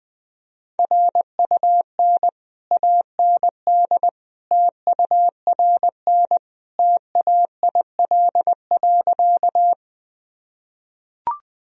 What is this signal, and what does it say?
Morse code, heard as a keyed tone, about 20 wpm, 700 Hz. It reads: RUN AND TURN TAIL.
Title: Sentences from Top 1000 Words - Review Part 3 - 20wpm